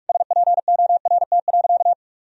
Morse code, heard as a keyed tone, about 41 wpm, 700 Hz. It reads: SPORT.
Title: Sentences from Top 1000 Words - Review Part 4 - 45wpm